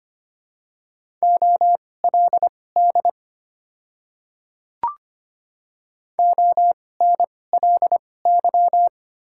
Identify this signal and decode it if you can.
Morse code, heard as a keyed tone, about 25 wpm, spaced as for 18 wpm, 700 Hz. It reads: OLD ONLY
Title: 2x Character Spacing - Top 200 Words - 25wpm